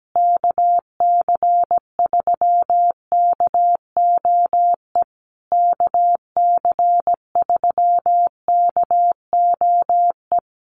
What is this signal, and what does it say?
Morse code, heard as a keyed tone, about 17 wpm, 700 Hz. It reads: KC3KOE KC3KOE
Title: USA Call Signs Repeated 3x 17wpm